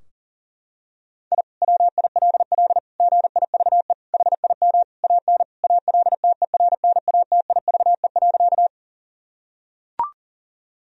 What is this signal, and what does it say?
Morse code, heard as a keyed tone, about 40 wpm, 700 Hz. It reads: I WILL GIVE HIM AN ALTERNATIVE.
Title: Sentences from Top 1000 Words - Review Part 4 - 40wpm